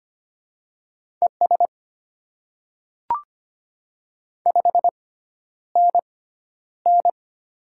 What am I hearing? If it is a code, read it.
Morse code, heard as a keyed tone, about 25 wpm, spaced as for 11 wpm, 700 Hz. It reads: ES 5NN
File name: Top 20 QSO Elements - 6x Character Spacing - 25wpm